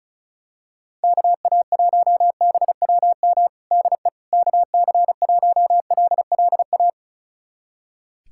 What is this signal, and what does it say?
Morse code, heard as a keyed tone, about 36 wpm, 700 Hz. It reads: KA1BWM DE KC1LLA
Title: USA Callsign DE Callsign 35wpm